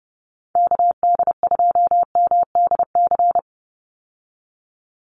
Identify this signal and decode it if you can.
Morse code, heard as a keyed tone, about 30 wpm, 700 Hz. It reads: KD2MDC
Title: USA Call Signs 30wpm